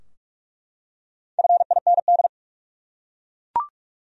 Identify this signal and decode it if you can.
Morse code, heard as a keyed tone, about 46 wpm, 700 Hz. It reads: FIND
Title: Top 100 Words 45wpm